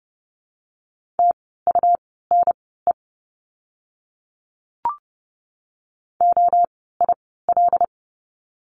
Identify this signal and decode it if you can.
Morse code, heard as a keyed tone, about 30 wpm, spaced as for 17 wpm, 700 Hz. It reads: TUNE OIL
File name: Lesson 17f - Letter u - Words 30wpm